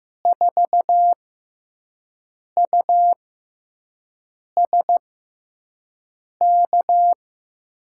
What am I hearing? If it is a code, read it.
Morse code, heard as a keyed tone, about 15 wpm, spaced as for 5 wpm, 700 Hz. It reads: 4USK
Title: USA Call Signs - 6 Characters - 6x Character Spacing - 15wpm